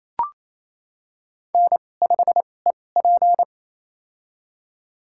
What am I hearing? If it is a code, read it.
Morse code, heard as a keyed tone, about 28 wpm, spaced as for 20 wpm, 700 Hz. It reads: N5EP
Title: USA Call Signs - 4 Characters - 2x Character Spacing - 28wpm